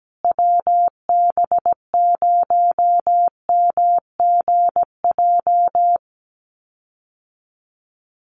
Morse code, heard as a keyed tone, about 17 wpm, 700 Hz. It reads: WB0MGJ
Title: USA Call Signs Repeated 2x 17wpm